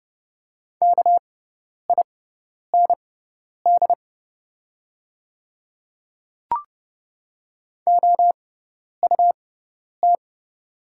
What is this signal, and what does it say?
Morse code, heard as a keyed tone, about 30 wpm, spaced as for 10 wpm, 690 Hz. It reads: KIND OUT
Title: 6x Character Spacing - Top 200 Words - 30wpm